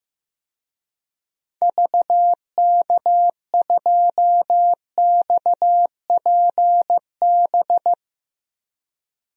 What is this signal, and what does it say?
Morse code, heard as a keyed tone, about 15 wpm, 700 Hz. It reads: VK2XPB
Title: DXCC-100 Call Signs - 6 Characters - 15wpm